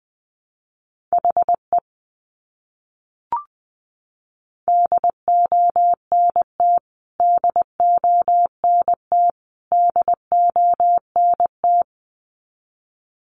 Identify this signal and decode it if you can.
Morse code, heard as a keyed tone, about 20 wpm, 700 Hz. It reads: HE DONT DONT DONT
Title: Top 500 Words Repeated 20wpm